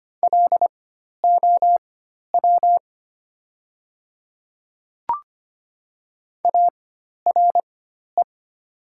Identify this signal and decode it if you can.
Morse code, heard as a keyed tone, about 25 wpm, spaced as for 12 wpm, 700 Hz. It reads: LOW ARE
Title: 4x Character Spacing - Top 500 Words - 25wpm